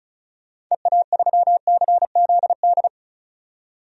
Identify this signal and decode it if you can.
Morse code, heard as a keyed tone, about 36 wpm, 700 Hz. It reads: EA3CZD
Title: DXCC-100 Call Signs - 6 Characters - 35wpm